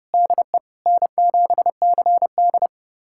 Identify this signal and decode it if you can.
Morse code, heard as a keyed tone, about 30 wpm, 700 Hz. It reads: DE N7CD
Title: USA Callsign DE Callsign - 4 Characters - 30wpm